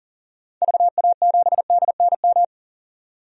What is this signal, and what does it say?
Morse code, heard as a keyed tone, about 40 wpm, 700 Hz. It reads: VA7DNM
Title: DXCC-100 Call Signs - 6 Characters - 40wpm